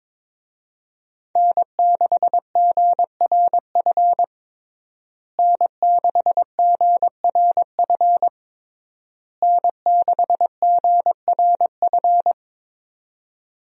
Morse code, heard as a keyed tone, about 23 wpm, 700 Hz. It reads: N6GRF N6GRF N6GRF
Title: USA Call Signs - 5 Characters - Repeated 3x - 22wpm